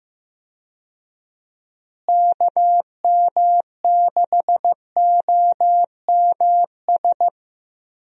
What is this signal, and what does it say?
Morse code, heard as a keyed tone, about 15 wpm, 700 Hz. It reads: KM6OMS
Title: USA Call Signs - 6 Characters - Repeated 2x - 15wpm